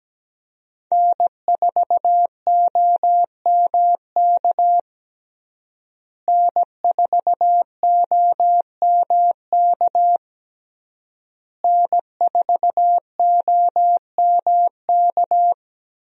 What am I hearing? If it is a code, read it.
Morse code, heard as a keyed tone, about 17 wpm, 700 Hz. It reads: N4OMK N4OMK N4OMK
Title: USA Call Signs - 5 Characters - Repeated 3x - 17wpm